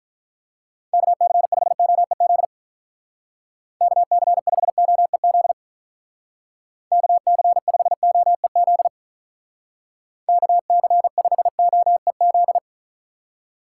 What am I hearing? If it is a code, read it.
Morse code, heard as a keyed tone, about 46 wpm, 700 Hz. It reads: KC5OEZ KC5OEZ KC5OEZ KC5OEZ